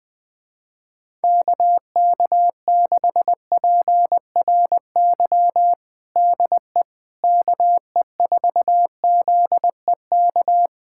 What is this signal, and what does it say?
Morse code, heard as a keyed tone, about 21 wpm, 700 Hz. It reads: KK6PRY DE KE4ZEK
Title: USA Callsign DE Callsign - 6 Characters - 20wpm